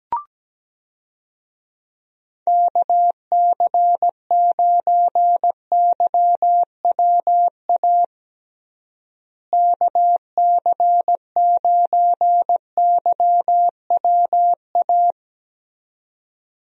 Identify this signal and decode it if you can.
Morse code, heard as a keyed tone, about 17 wpm, 700 Hz. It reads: KC9YWA KC9YWA
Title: USA Call Signs - 6 Characters - Repeated 2x - 17wpm